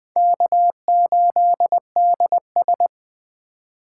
Morse code, heard as a keyed tone, about 20 wpm, 700 Hz. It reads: K8DS
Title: Speed Racing - USA 1x2 Call Signs 17wpm